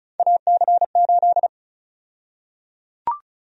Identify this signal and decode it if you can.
Morse code, heard as a keyed tone, about 34 wpm, 700 Hz. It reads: AC8
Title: USA Call Sign Prefixes - Repeated 2x - 35wpm